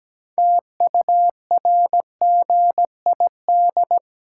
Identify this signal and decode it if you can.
Morse code, heard as a keyed tone, about 17 wpm, 700 Hz. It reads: TURGID
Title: Pangram Challenge v2 - 17wpm